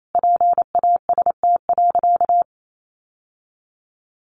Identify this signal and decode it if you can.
Morse code, heard as a keyed tone, about 28 wpm, 700 Hz. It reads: PAST.
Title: Sentences from Top 1100 Words 28wpm